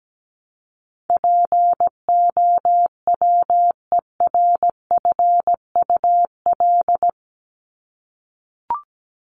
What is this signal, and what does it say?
Morse code, heard as a keyed tone, about 17 wpm, 700 Hz. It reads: POWERFUL
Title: Top 1251-1500 Words 17wpm